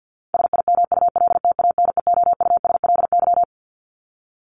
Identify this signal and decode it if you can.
Morse code, heard as a keyed tone, about 50 wpm, 700 Hz. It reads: SIMULTANEOUSLY
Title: LY Suffix - Top 100 Words with Base Words 50wpm